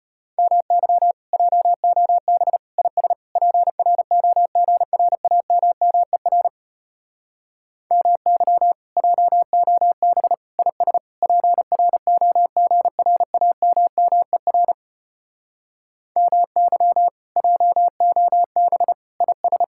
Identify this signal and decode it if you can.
Morse code, heard as a keyed tone, about 33 wpm, 700 Hz. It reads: MY JOB IS PROGRAMMER MY JOB IS PROGRAMMER MY JOB IS